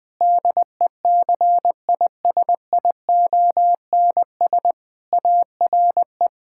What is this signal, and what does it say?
Morse code, heard as a keyed tone, about 20 wpm, 700 Hz. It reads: DECISIONS ARE
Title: Sentences with 4-Word Phrases - Spoken Beginning - 20wpm